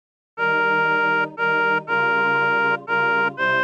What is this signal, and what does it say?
There is no code.